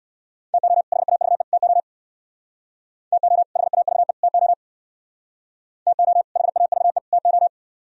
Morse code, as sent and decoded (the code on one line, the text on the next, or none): - --- / .... .- ...- . / - --- / - --- / .... .- ...- . / - --- / - --- / .... .- ...- . / - ---
TO HAVE TO TO HAVE TO TO HAVE TO